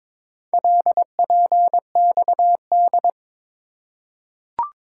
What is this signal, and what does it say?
Morse code, heard as a keyed tone, about 21 wpm, 700 Hz. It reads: LPXD
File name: practice / 22wpm / Four Letter-Number Combinations - 22wpm